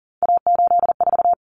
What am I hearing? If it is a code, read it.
Morse code, heard as a keyed tone, about 40 wpm, 700 Hz. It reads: A84
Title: Common Cut Numbers - 3-Digit 40wpm